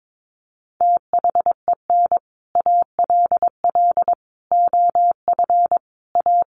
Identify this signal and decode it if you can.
Morse code, heard as a keyed tone, about 22 wpm, 700 Hz. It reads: THEN ALL OF A
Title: Sets of 5 Words Encore 22wpm